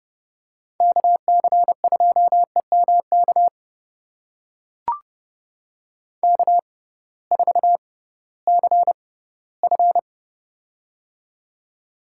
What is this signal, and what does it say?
Morse code, heard as a keyed tone, about 31 wpm, spaced as for 11 wpm, 700 Hz. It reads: KC2EM KK4CF
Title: USA Call Signs - 6x Character Spacing - 30wpm